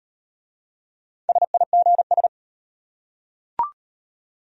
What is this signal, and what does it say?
Morse code, heard as a keyed tone, about 34 wpm, 700 Hz. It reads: SIGS